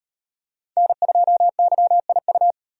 Morse code, heard as a keyed tone, about 34 wpm, 700 Hz. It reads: N2YIU